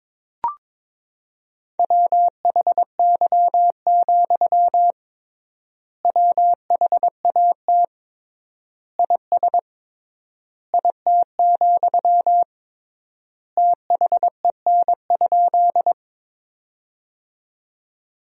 Morse code, heard as a keyed tone, about 22 wpm, 700 Hz. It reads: WHY, WHAT IS IT, THEN?
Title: 3x Word-Spacing - Sentences from Top 200 Words - 22wpm